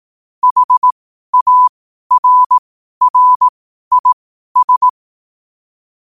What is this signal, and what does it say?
Morse code, heard as a keyed tone, about 18 wpm, spaced as for 12 wpm, 1000 Hz. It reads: HARRIS
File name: qotdfiles / 12wpm